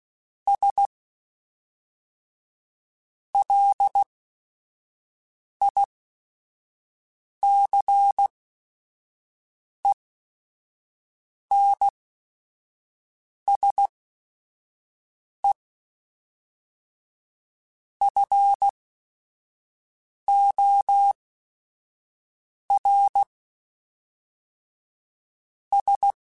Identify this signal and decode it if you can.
Morse code, heard as a keyed tone, about 16 wpm, spaced as for 5 wpm, 800 Hz. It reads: S LICENSE FOR S